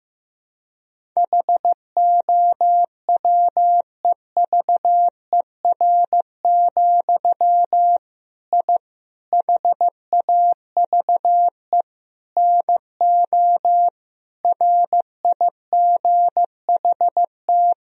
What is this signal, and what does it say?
Morse code, heard as a keyed tone, about 16 wpm, 700 Hz. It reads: HOWEVER, I HAVE NO RIGHT
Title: Sentences from Top 1000 Words - Review Part 3 - 15wpm